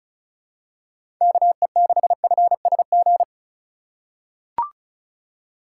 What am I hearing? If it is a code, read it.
Morse code, heard as a keyed tone, about 36 wpm, 700 Hz. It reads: KE6FSG
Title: USA Call Signs - 6 Characters - 4x Character Spacing - 35wpm